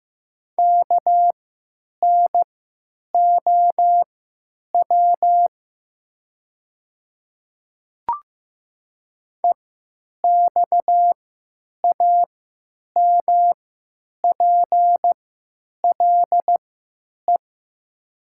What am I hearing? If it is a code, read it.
Morse code, heard as a keyed tone, about 15 wpm, spaced as for 9 wpm, 700 Hz. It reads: KNOW EXAMPLE
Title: 3x Character Spacing - Top 500 Words - 15wpm